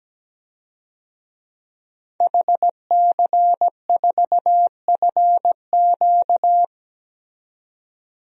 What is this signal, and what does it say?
Morse code, heard as a keyed tone, about 17 wpm, 700 Hz. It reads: HC4FQ
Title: DXCC-100 Call Signs - 5 Characters - 17wpm